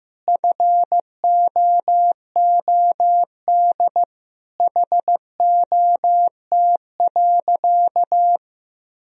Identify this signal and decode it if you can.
Morse code, heard as a keyed tone, about 15 wpm, 700 Hz. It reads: FOOD HOT.
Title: Sentences from Top 1000 Words - Review Part 3 - 15wpm